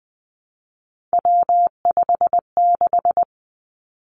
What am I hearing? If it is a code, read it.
Morse code, heard as a keyed tone, about 19 wpm, 700 Hz. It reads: W56
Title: Three Challenging Letter-Number Combinations 20wpm